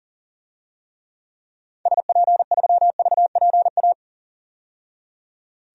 Morse code, heard as a keyed tone, about 40 wpm, 700 Hz. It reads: SP3VPA
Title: DXCC-100 Call Signs - 6 Characters - 40wpm